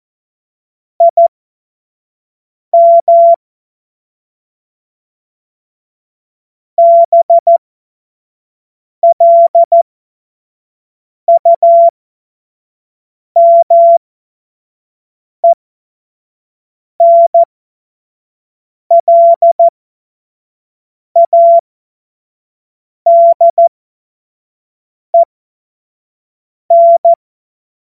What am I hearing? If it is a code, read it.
Morse code, heard as a keyed tone, about 14 wpm, spaced as for 5 wpm, 680 Hz. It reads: IM BLUMENLADEN
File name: Morsekurs / Witze / Witz 13 Februar 2024 25 F70